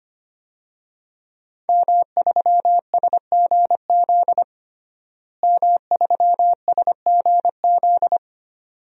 Morse code, heard as a keyed tone, about 25 wpm, 700 Hz. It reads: M3SGZ M3SGZ
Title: DXCC-100 Call Signs - 5 Characters - Repeated 2x - 25wpm